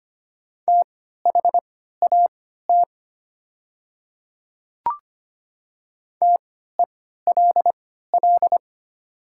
Morse code, heard as a keyed tone, about 25 wpm, spaced as for 14 wpm, 700 Hz. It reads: THAT TELL
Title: 3x Character Spacing - Top 200 Words - 25wpm